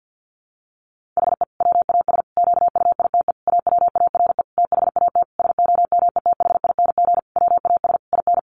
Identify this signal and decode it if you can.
Morse code, heard as a keyed tone, about 50 wpm, 700 Hz. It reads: HE WAS QUITE AWARE THAT SOMETHING WAS IN
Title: Sentences from Top 1400 Words 50wpm